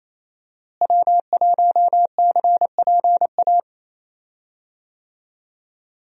Morse code, heard as a keyed tone, about 28 wpm, 700 Hz. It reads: W1CPA
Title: USA Call Signs - 5 Characters - Repeated 2x - 28wpm